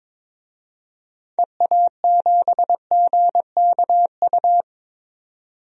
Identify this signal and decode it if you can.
Morse code, heard as a keyed tone, about 22 wpm, 700 Hz. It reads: EA7GKU
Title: DXCC-100 Call Signs - 6 Characters - 22wpm